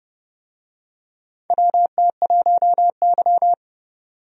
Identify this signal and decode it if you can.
Morse code, heard as a keyed tone, about 32 wpm, 700 Hz. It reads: WT1Y
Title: USA Call Signs - 4 Characters - 30wpm